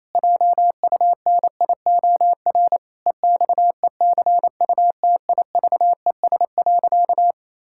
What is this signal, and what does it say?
Morse code, heard as a keyed tone, about 27 wpm, 700 Hz. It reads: JUNIOR EXECUTIVES.
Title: Pangram Challenge v2 - 28wpm